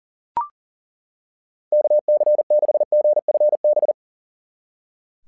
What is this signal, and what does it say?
Morse code, heard as a keyed tone, about 40 wpm, 600 Hz. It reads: KC6GFB
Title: SOTA QSO Practice 40wpm